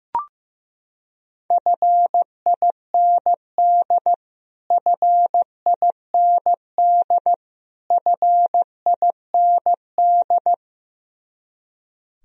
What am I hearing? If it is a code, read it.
Morse code, heard as a keyed tone, about 15 wpm, 700 Hz. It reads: FIND FIND FIND
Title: Top 100 Words Repeated 15wpm